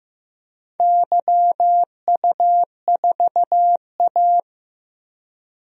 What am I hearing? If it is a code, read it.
Morse code, heard as a keyed tone, about 15 wpm, 700 Hz. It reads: YU4A
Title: DXCC-100 Call Signs - 4 Characters - Repeated 3x - 15wpm